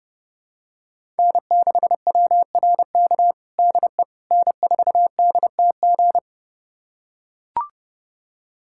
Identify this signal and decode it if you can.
Morse code, heard as a keyed tone, about 30 wpm, 700 Hz. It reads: N6WRK DE N4DTG
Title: USA Callsign DE Callsign 30wpm